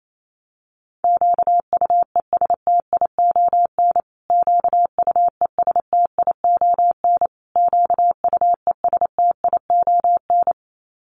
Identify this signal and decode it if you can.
Morse code, heard as a keyed tone, about 28 wpm, 700 Hz. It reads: QUESTION QUESTION QUESTION
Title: Top 500 Words Repeated 28wpm